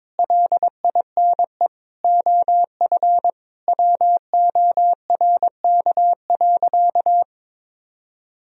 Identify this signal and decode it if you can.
Morse code, heard as a keyed tone, about 22 wpm, 700 Hz. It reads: LINE OF WORK.
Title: Sentences from Top 1000 Words - Review Part 4 - 22wpm